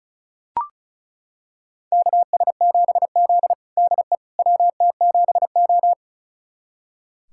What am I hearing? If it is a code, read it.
Morse code, heard as a keyed tone, about 36 wpm, 700 Hz. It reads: KS7Z DE WT7O
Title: USA Callsign DE Callsign - 4 Characters - 35wpm